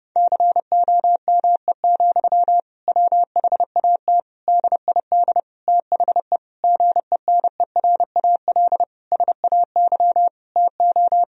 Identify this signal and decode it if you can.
Morse code, heard as a keyed tone, about 30 wpm, 700 Hz. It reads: COME, WHAT DID THE GENERAL SAY TO
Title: Sentences from Top 1000 Words - Review Part 3 - 30wpm